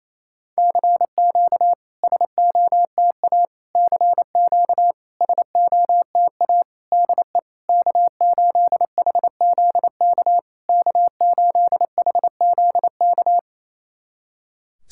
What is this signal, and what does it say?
Morse code, heard as a keyed tone, about 28 wpm, 700 Hz. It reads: CQ SOTA CQ SOTA DE K8HZK K8HZK
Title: SOTA QSO Practice 28wpm